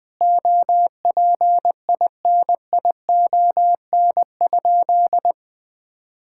Morse code, heard as a keyed tone, about 20 wpm, 700 Hz. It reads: OPINION?
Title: Sentences from Top 1600 Words 20wpm